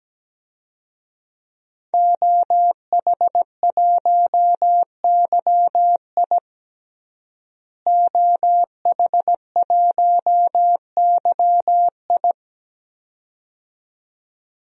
Morse code, heard as a keyed tone, about 17 wpm, 700 Hz. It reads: OH1YI OH1YI
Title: DXCC-100 Call Signs - 5 Characters - Repeated 2x - 17wpm